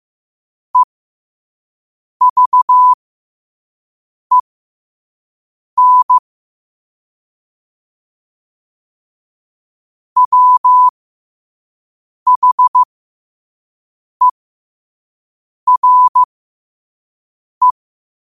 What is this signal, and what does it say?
Morse code, heard as a keyed tone, about 15 wpm, spaced as for 5 wpm, 1000 Hz. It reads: EVEN WHERE
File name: qotdfiles / 5wpm